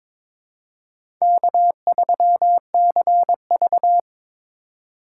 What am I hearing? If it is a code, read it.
Morse code, heard as a keyed tone, about 22 wpm, 700 Hz. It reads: K3CV
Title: USA Call Signs - 4 Characters - 2x Character Spacing - 22wpm